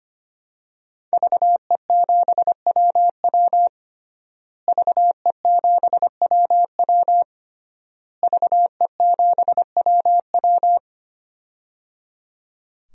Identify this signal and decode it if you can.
Morse code, heard as a keyed tone, about 25 wpm, 700 Hz. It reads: VE7WW VE7WW VE7WW
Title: DXCC-100 Call Signs - 5 Characters - Repeated 3x - 25wpm